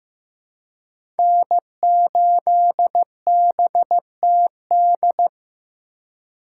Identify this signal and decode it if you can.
Morse code, heard as a keyed tone, about 15 wpm, 700 Hz. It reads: N8BTD
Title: USA Call Signs - 5 Characters - 4x Character Spacing - 15wpm